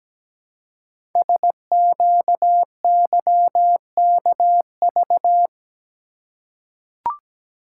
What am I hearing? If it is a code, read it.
Morse code, heard as a keyed tone, about 17 wpm, 700 Hz. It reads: SQYKV
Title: Code Groups - 5 Letters - 17wpm